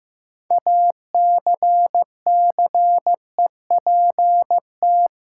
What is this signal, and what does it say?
Morse code, heard as a keyed tone, about 15 wpm, 700 Hz. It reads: ACCEPT
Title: Sentences from Top 1300 Words - New Word Drill - 15wpm